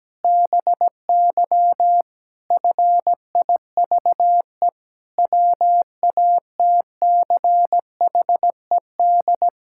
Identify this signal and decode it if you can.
Morse code, heard as a keyed tone, about 17 wpm, 700 Hz. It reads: BY FIVE WATCHED
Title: Pangram Challenge v2 - 17wpm